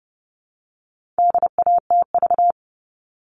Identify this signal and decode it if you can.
Morse code, heard as a keyed tone, about 30 wpm, 700 Hz. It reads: DATV
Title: All Cut Numbers - 4-Digit 30wpm